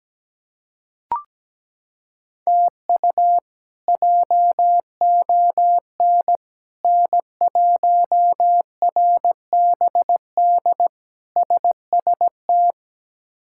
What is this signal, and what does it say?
Morse code, heard as a keyed tone, about 17 wpm, 700 Hz. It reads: TU JON N1RBD SST
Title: K1USN SST QSO Practice - 17wpm